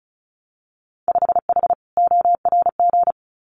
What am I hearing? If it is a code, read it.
Morse code, heard as a keyed tone, about 36 wpm, 700 Hz. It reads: 5H ORG
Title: Winter Field Day Exchanges Encore 35wpm